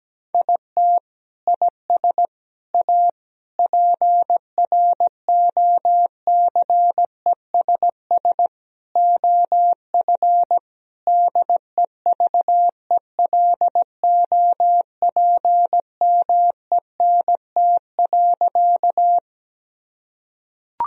Morse code, heard as a keyed tone, about 17 wpm, 700 Hz. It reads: IT IS A PROCESS OF DEVELOPMENT.
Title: Sentences from Top 1000 Words - Review Part 4 - 17wpm